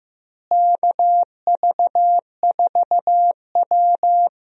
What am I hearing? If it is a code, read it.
Morse code, heard as a keyed tone, about 15 wpm, 700 Hz. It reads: KV4W
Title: USA Call Signs - 4 Characters - Repeated 3x - 15wpm